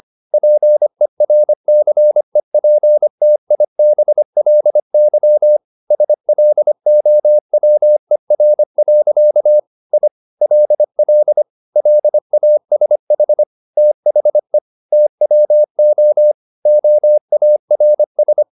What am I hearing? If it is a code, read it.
Morse code, heard as a keyed tone, about 25 wpm, 600 Hz. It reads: PERCEPTIBLY SLOWER. I LL LASH THE TWO OARS